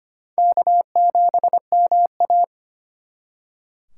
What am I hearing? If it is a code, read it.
Morse code, heard as a keyed tone, about 30 wpm, 700 Hz. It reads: K7MA
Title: Speed Racing - USA Call Signs 25wpm